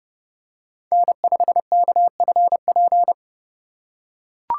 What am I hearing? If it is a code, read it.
Morse code, heard as a keyed tone, about 30 wpm, 700 Hz. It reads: N5KFP
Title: USA Call Signs - 5 Characters - 3x Character Spacing - 30wpm